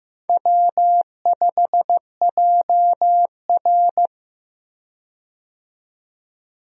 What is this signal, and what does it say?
Morse code, heard as a keyed tone, about 15 wpm, 700 Hz. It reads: W5JR
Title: USA Call Signs - 4 Characters - 15wpm